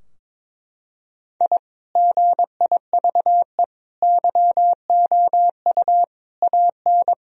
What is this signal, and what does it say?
Morse code, heard as a keyed tone, about 22 wpm, 700 Hz. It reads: I GIVE YOU AN
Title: Sentences from Top 1000 Words - Review Part 2 - 22wpm